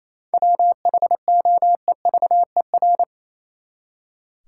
Morse code, heard as a keyed tone, about 28 wpm, 700 Hz. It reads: WHOEVER
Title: Seven Letter Words - 28wpm